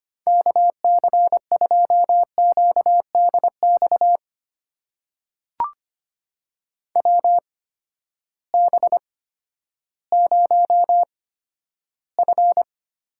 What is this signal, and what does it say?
Morse code, heard as a keyed tone, about 25 wpm, spaced as for 12 wpm, 700 Hz. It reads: KC2QDX WB0F
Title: USA Call Signs - 6 Characters - 8x Character Spacing - 25wpm